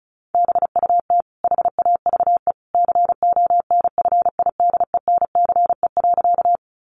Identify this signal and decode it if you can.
Morse code, heard as a keyed tone, about 34 wpm, 700 Hz. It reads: BUT HAVE CONFIDENCE.
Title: Sentences from Top 1500 Words 35wpm